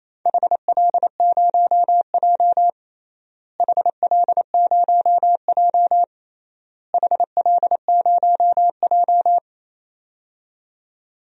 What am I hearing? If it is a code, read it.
Morse code, heard as a keyed tone, about 28 wpm, 700 Hz. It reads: HL0J HL0J HL0J
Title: DXCC-100 Call Signs - 4 Characters - Repeated 3x - 28wpm